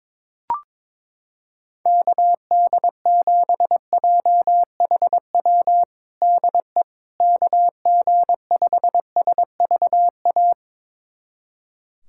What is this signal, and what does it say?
Morse code, heard as a keyed tone, about 22 wpm, 700 Hz. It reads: KD7JHW DE KG5SVA
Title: USA Callsign DE Callsign - 6 Characters - 22wpm